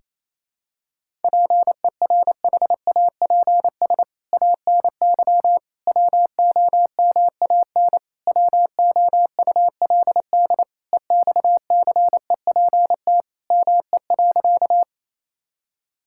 Morse code, heard as a keyed tone, about 28 wpm, 700 Hz. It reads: PERHAPS ANY WOMAN WOULD EXCEPT ME.
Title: Sentences from Top 1600 Words 28wpm